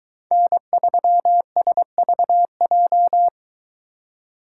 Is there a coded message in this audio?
Yes: Morse code from 0.3 to 3.3 s, heard as a keyed tone, about 23 wpm, 700 Hz. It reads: N3SVJ